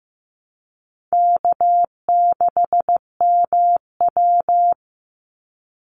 Morse code, heard as a keyed tone, about 15 wpm, 700 Hz. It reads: K6MW
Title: NA SOTA Chaser Callsigns 15wpm